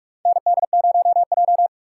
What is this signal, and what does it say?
Morse code, heard as a keyed tone, about 43 wpm, 700 Hz. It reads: ND0J
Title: USA Call Signs - 4 Characters - 4x Character Spacing - 45wpm